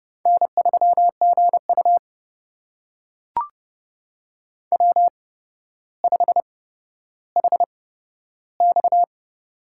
Morse code, heard as a keyed tone, about 30 wpm, spaced as for 13 wpm, 700 Hz. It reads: N3GU W5HX
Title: USA Call Signs - 4 Characters - 8x Character Spacing - 30wpm